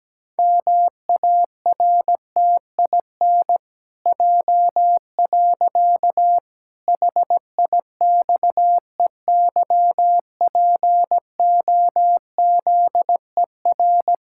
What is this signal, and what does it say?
Morse code, heard as a keyed tone, about 17 wpm, 700 Hz. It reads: MARTIN J. HIXEYPOZER
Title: Pangram Challenge v2 - 17wpm